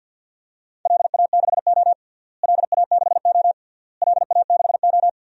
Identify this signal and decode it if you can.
Morse code, heard as a keyed tone, about 52 wpm, 700 Hz. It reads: LA6O LA6O LA6O
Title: DXCC-100 Call Signs - 4 Characters - Repeated 3x - 50wpm